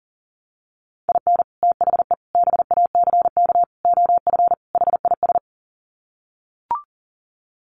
Morse code, heard as a keyed tone, about 40 wpm, 700 Hz. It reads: IN THE BACK OF HIS
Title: Sets of 5 Words Encore2 40wpm